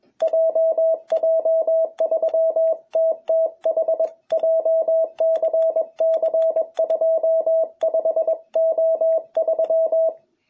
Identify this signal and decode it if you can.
Morse code, heard as a keyed tone, about 21 wpm, 660 Hz. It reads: JJ3TTH JCC25O3
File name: jh8cbh / 2021cw